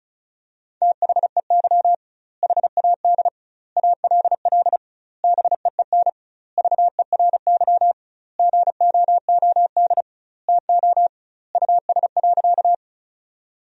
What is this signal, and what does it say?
Morse code, heard as a keyed tone, about 36 wpm, 700 Hz. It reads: THEY HAD ALL BEEN VERY GOOD TO US.
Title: 2x Word-Spacing - Sentences from Top 200 Words - 35wpm